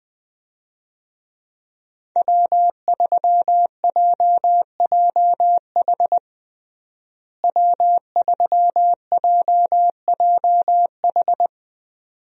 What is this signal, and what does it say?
Morse code, heard as a keyed tone, about 20 wpm, 700 Hz. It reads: W3JJH W3JJH
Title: USA Call Signs - 5 Characters - Repeated 2x - 20wpm